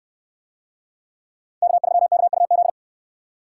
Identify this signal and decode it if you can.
Morse code, heard as a keyed tone, about 62 wpm, 700 Hz. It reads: K3KUZ